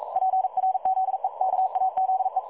Morse code, heard as a keyed tone, about 42 wpm, 760 Hz. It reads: PRZEKAZE
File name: dlazaawansowanych / nagraniazpasma / websdr recording 3565.0kHz